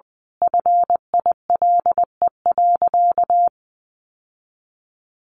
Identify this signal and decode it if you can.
Morse code, heard as a keyed tone, about 20 wpm, 700 Hz. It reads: FILE.
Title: Sentences from Top 1200 Words 20wpm